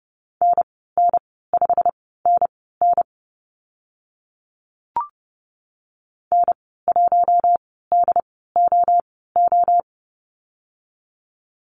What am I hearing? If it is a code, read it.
Morse code, heard as a keyed tone, about 30 wpm, spaced as for 17 wpm, 690 Hz. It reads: NN5NN N1DOO
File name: Lesson 15g - Number 5 - Call Signs 30wpm